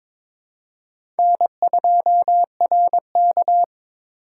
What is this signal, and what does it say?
Morse code, heard as a keyed tone, about 22 wpm, 700 Hz. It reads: N2RK
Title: USA Call Signs - 4 Characters - 4x Character Spacing - 22wpm